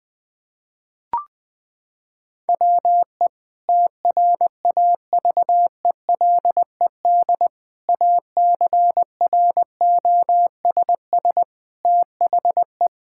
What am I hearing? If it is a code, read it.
Morse code, heard as a keyed tone, about 20 wpm, 700 Hz. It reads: WE TRAVELED ACROSS THE
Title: Sentences with 2-Word Phrases - 20wpm